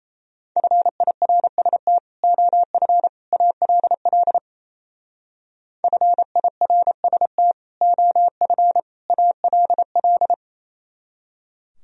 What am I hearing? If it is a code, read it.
Morse code, heard as a keyed tone, about 34 wpm, 700 Hz. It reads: FIRST OF ALL FIRST OF ALL
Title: Speed-Racing - Sets of 3 Words 28wpm